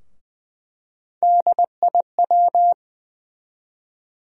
Morse code, heard as a keyed tone, about 20 wpm, 700 Hz. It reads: DIW
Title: USA Call Sign Postfixes - 3 Letters 20wpm